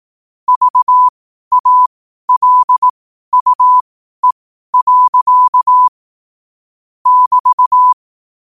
Morse code, heard as a keyed tone, about 18 wpm, spaced as for 12 wpm, 1000 Hz. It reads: VALUE. <BT>
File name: qotdfiles / 12wpm